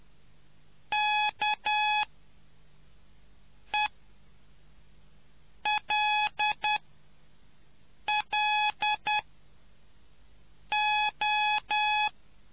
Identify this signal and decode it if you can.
Morse code, heard as a keyed tone, about 10 wpm, spaced as for 4 wpm, 840 Hz. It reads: KELLO